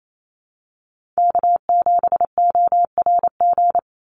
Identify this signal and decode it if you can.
Morse code, heard as a keyed tone, about 28 wpm, 700 Hz. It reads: K7ORG
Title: NA SOTA Chaser Callsigns 28wpm